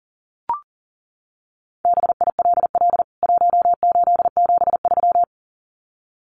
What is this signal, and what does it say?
Morse code, heard as a keyed tone, about 37 wpm, 700 Hz. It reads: BILL 1873
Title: CWT Exchange 40wpm